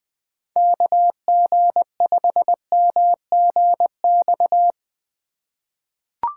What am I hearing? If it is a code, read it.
Morse code, heard as a keyed tone, about 20 wpm, 700 Hz. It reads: KG5MGX
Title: USA Call Signs - 3x Character Spacing - 20wpm